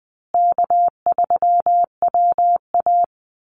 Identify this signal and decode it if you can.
Morse code, heard as a keyed tone, about 20 wpm, 700 Hz. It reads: K3WA
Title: Call Signs Repeated 3x - 20wpm